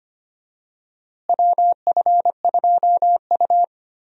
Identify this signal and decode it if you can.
Morse code, heard as a keyed tone, about 25 wpm, 700 Hz. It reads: WF2U
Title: USA Call Signs - 4 Characters - Repeated 3x - 25wpm